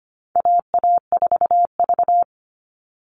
Morse code, heard as a keyed tone, about 25 wpm, 700 Hz. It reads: AA4V
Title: All Cut Numbers - 4-Digit 25wpm